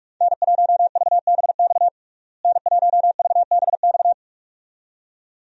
Morse code, heard as a keyed tone, about 46 wpm, 700 Hz. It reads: N1VBX N1VBX
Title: USA Call Signs - 5 Characters - Repeated 2x - 45wpm